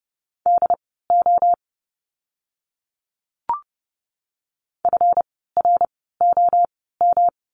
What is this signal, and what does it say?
Morse code, heard as a keyed tone, about 30 wpm, spaced as for 17 wpm, 700 Hz. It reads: DO FROM
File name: course / 30:10wpm / 230 Lesson 37f - Letter x - Words 30wpm